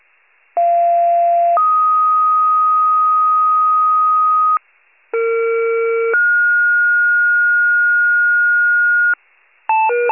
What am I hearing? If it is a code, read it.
Two-tone page, heard as a one-second tone, then a three-second tone, about 690 then 1250 Hz, about 460 then 1430 Hz, with noise. It is heard from 0.6 to 9.1 s.